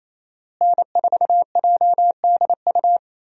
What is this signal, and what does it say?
Morse code, heard as a keyed tone, about 28 wpm, 700 Hz. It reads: N4JDU
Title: USA Call Signs - 5 Characters - Repeated 2x - 28wpm